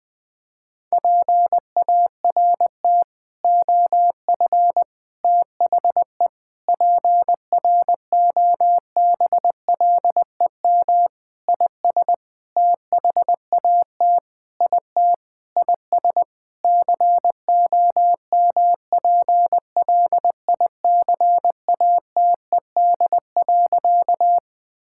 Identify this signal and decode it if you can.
Morse code, heard as a keyed tone, about 20 wpm, 700 Hz. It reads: PART OF THE PROBLEM IS THAT IT IS COMPLICATED.
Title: Sentences with 6-Word Phrases - 20wpm